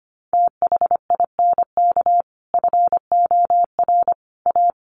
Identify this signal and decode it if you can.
Morse code, heard as a keyed tone, about 25 wpm, 700 Hz. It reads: THINK FOR A
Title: Sentences from Top 1000 Words 25wpm